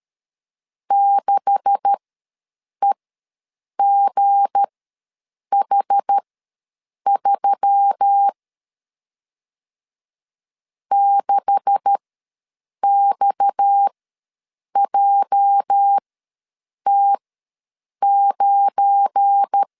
Morse code, heard as a keyed tone, about 13 wpm, spaced as for 7 wpm, 800 Hz. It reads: 6EGH3 6XJT9
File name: LEZIONI / 9 286 ws 12wpm 3351